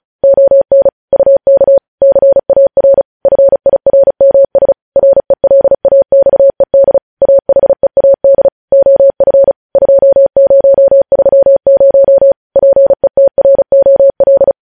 Morse code, heard as a keyed tone, about 36 wpm, 550 Hz. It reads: ON UK CAR FIRMS RELAXED AHEAD OF 2030 PETROL
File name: podcast / News headlines-1x-35x35